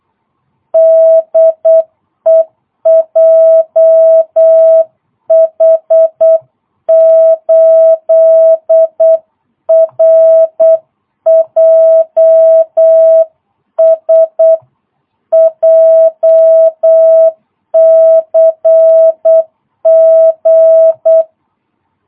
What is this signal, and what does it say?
Morse code, heard as a keyed tone, about 8 wpm, 650 Hz. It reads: DE JH8RJS JCG
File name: jh8cbh / 2022cw